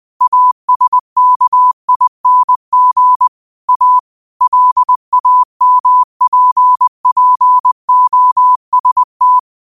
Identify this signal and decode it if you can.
Morse code, heard as a keyed tone, about 20 wpm, 1000 Hz. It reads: ASKING A LAMPPOST
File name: qotdfiles / 20wpm